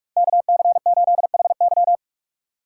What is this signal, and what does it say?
Morse code, heard as a keyed tone, about 44 wpm, 700 Hz. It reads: KC8HY